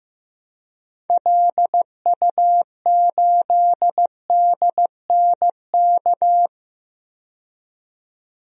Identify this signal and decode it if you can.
Morse code, heard as a keyed tone, about 15 wpm, 700 Hz. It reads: LU8DNK
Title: DXCC-100 Call Signs - 6 Characters - 15wpm